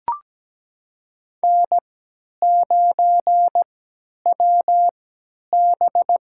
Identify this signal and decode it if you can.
Morse code, heard as a keyed tone, about 17 wpm, spaced as for 10 wpm, 700 Hz. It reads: N9WB
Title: USA Call Signs - 4 Characters - 3x Character Spacing - 17wpm